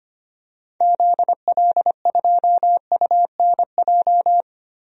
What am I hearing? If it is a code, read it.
Morse code, heard as a keyed tone, about 25 wpm, 700 Hz. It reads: ZL2UNJ
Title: DXCC-100 Call Signs - 6 Characters - Repeated 3x - 25wpm